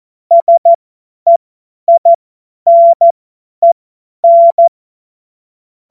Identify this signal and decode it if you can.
Morse code, heard as a keyed tone, about 14 wpm, spaced as for 10 wpm, 680 Hz. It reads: SEINEN
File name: Witz-07-Oktober-2025-50-F70